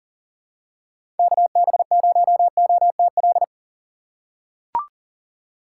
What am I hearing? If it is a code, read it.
Morse code, heard as a keyed tone, about 40 wpm, 700 Hz. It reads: KB0OTL
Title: USA Call Signs - 4x Character Spacing - 40wpm